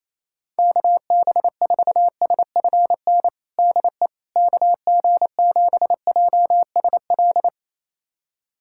Morse code, heard as a keyed tone, about 27 wpm, 700 Hz. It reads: KB4SFN DE KG7JSL
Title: USA Callsign DE Callsign - 6 Characters - 28wpm